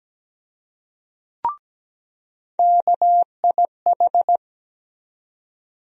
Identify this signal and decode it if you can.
Morse code, heard as a keyed tone, about 17 wpm, 700 Hz. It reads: KIH
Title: USA Call Sign Postfixes - 3 Letters 17wpm